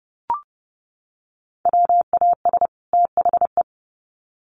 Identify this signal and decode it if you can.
Morse code, heard as a keyed tone, about 30 wpm, 700 Hz. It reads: WAS THE
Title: Sets of 2 Words 30wpm